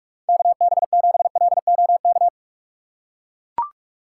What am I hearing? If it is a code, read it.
Morse code, heard as a keyed tone, about 43 wpm, 700 Hz. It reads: KB7LOK
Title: USA Call Signs - 6 Characters - 3x Character Spacing - 45wpm